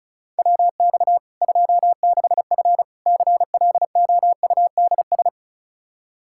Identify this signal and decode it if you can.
Morse code, heard as a keyed tone, about 36 wpm, 700 Hz. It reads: WX 26F CLOUDS
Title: Practice QSO - Exchange 2 - Info - 35wpm